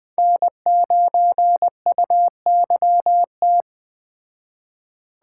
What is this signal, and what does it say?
Morse code, heard as a keyed tone, about 21 wpm, 700 Hz. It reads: N9UYT
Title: Speed Racing - USA Call Signs 17wpm